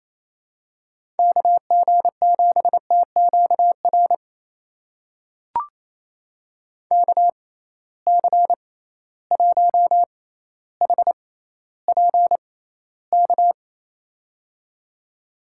Morse code, heard as a keyed tone, about 28 wpm, spaced as for 14 wpm, 700 Hz. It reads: KG7TQR KC1HPK
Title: USA Call Signs - 6 Characters - 6x Character Spacing - 28wpm